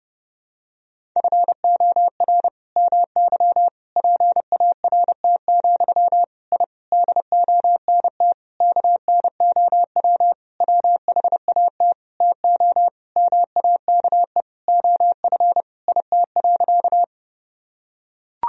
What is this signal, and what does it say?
Morse code, heard as a keyed tone, about 30 wpm, 700 Hz. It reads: FOR MY PART, I DONT KNOW WHAT TO MAKE OF IT.
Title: Sentences from Top 1000 Words - Review Part 1 - 30wpm